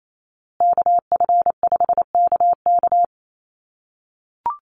Morse code, heard as a keyed tone, about 28 wpm, 700 Hz. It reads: KF5KK
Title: USA Call Signs Repeated 3x 28wpm